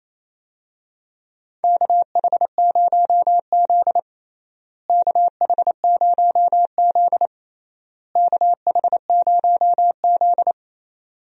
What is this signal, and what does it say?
Morse code, heard as a keyed tone, about 28 wpm, 700 Hz. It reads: KH0Z KH0Z KH0Z
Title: USA Call Signs - 4 Characters - Repeated 3x - 28wpm